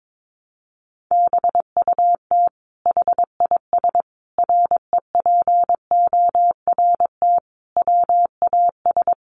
Morse code, heard as a keyed tone, about 22 wpm, 700 Hz. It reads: BUT HIS REPORT WAS
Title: Sentences from Top 1200 Words 22wpm